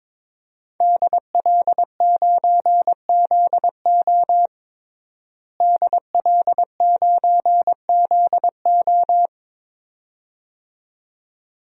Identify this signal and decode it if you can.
Morse code, heard as a keyed tone, about 22 wpm, 700 Hz. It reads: DL9ZO DL9ZO
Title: DXCC-100 Call Signs - 5 Characters - Repeated 2x - 22wpm